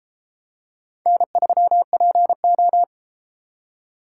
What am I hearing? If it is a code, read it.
Morse code, heard as a keyed tone, about 31 wpm, 700 Hz. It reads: N3PO